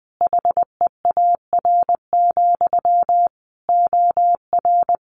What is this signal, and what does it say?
Morse code, heard as a keyed tone, about 20 wpm, 700 Hz. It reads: HEAR, OR
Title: Sentences from Top 1300 Words 20wpm